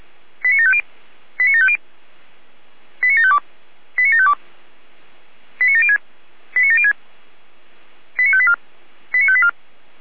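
Five-tone selective call (ZVEI), heard as quick runs of five tones, tones of about 70 ms. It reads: 89750 89750 89752 89752 89706 89706 89584 89584
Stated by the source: ZVEI five-tone selective call received on an SDR.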